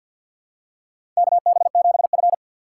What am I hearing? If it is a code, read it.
Morse code, heard as a keyed tone, about 47 wpm, 700 Hz. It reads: KB7F
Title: USA Call Signs - 4 Characters - 2x Character Spacing - 50wpm